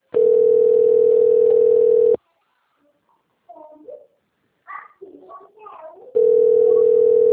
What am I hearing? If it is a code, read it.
Ringback tone (North America), 440 + 480 Hz.